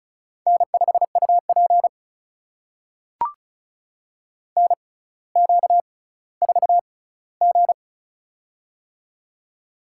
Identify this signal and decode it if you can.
Morse code, heard as a keyed tone, about 36 wpm, spaced as for 18 wpm, 700 Hz. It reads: N5UP NQ4G
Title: USA Call Signs - 4 Characters - 6x Character Spacing - 35wpm